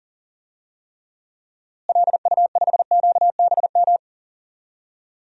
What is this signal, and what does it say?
Morse code, heard as a keyed tone, about 40 wpm, 700 Hz. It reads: LU5QBM
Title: DXCC-100 Call Signs - 6 Characters - Repeated 2x - 40wpm